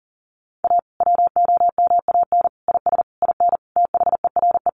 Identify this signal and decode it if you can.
Morse code, heard as a keyed tone, about 37 wpm, 700 Hz. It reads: A WOMAN IS IN THERE
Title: Sentences from Top 1400 Words 40wpm